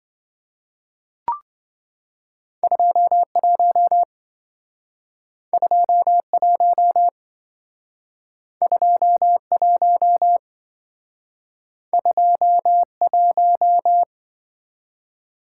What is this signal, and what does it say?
Morse code, heard as a keyed tone, about 27 wpm, 700 Hz. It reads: 21 21 21 21